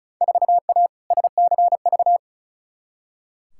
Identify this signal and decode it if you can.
Morse code, heard as a keyed tone, about 36 wpm, 700 Hz. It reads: 4A SCV
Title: Field Day Exchanges Encore - 35wpm